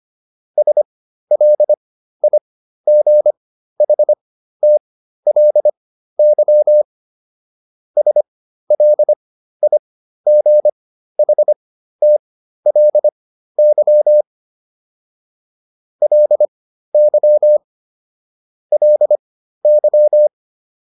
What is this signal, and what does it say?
Morse code, heard as a keyed tone, about 25 wpm, spaced as for 13 wpm, 600 Hz. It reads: SLIGHTLY SLIGHTLY LY LY